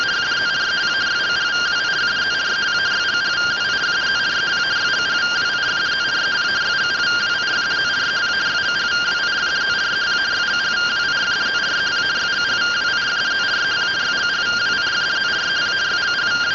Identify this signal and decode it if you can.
Two-tone teleprinter signal, about 170 Hz shift.